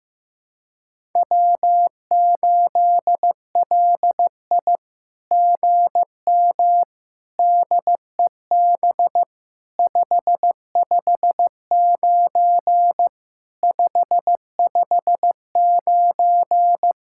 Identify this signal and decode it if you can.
Morse code, heard as a keyed tone, about 15 wpm, 700 Hz. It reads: W8LI GM DEB 559 559
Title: SOTA QSO Practice 15wpm